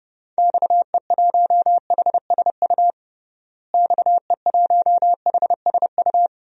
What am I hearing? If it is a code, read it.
Morse code, heard as a keyed tone, about 32 wpm, 700 Hz. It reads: XE1HSU XE1HSU
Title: DXCC-100 Call Signs - 6 Characters - Repeated 3x - 30wpm